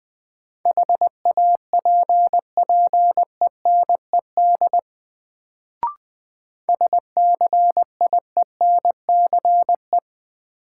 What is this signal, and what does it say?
Morse code, heard as a keyed tone, about 20 wpm, 700 Hz. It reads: HAPPENED SCIENCE
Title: Sentences from Top 1000 Words - Review Part 2 - New Word Drill - 20wpm